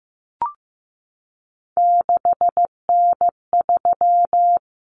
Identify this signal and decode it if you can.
Morse code, heard as a keyed tone, about 15 wpm, 700 Hz. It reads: 6N3
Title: Common Cut Numbers - 3-Digit 15wpm